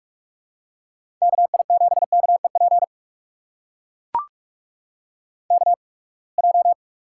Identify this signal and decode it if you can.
Morse code, heard as a keyed tone, about 44 wpm, spaced as for 30 wpm, 700 Hz. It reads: KI7KEP KJ
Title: USA Call Signs - 6 Characters - 8x Character Spacing - 45wpm